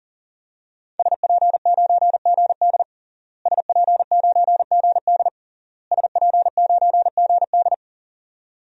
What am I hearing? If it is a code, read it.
Morse code, heard as a keyed tone, about 40 wpm, 700 Hz. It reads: SP9GD SP9GD SP9GD
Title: DXCC-100 Call Signs - 5 Characters - Repeated 3x - 40wpm